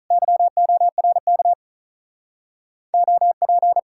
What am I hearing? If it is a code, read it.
Morse code, heard as a keyed tone, about 46 wpm, 700 Hz. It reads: YORK OP